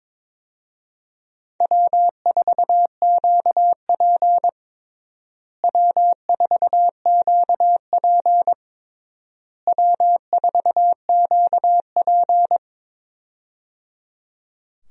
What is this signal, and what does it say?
Morse code, heard as a keyed tone, about 22 wpm, 700 Hz. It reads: W4QP W4QP W4QP
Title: USA Call Signs - 4 Characters - Repeated 3x - 22wpm